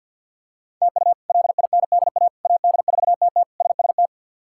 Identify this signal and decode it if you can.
Morse code, heard as a keyed tone, about 46 wpm, 700 Hz. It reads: TU LINDA AD4TT SST
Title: K1USN SST QSO Practice - 50wpm